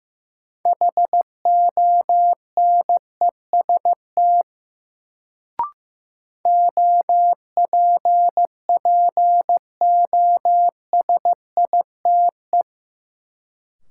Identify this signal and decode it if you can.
Morse code, heard as a keyed tone, about 15 wpm, 700 Hz. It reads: HONEST OPPOSITE
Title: Sets of 5 Words Encore - New Word Drill - 15wpm